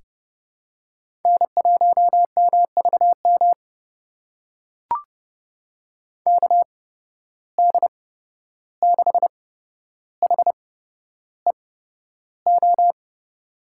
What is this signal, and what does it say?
Morse code, heard as a keyed tone, about 31 wpm, spaced as for 12 wpm, 700 Hz. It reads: N1MVM KD6HEO
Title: USA Call Signs - 8x Character Spacing - 30wpm